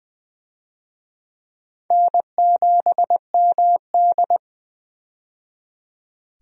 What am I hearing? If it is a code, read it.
Morse code, heard as a keyed tone, about 20 wpm, 700 Hz. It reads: N7MD
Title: USA Call Signs - 4 Characters - 20wpm